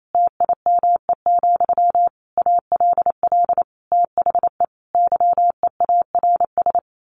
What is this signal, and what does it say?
Morse code, heard as a keyed tone, about 28 wpm, 700 Hz. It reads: TIME, ALL THE YEARS
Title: A Sound Of Thunder 28wpm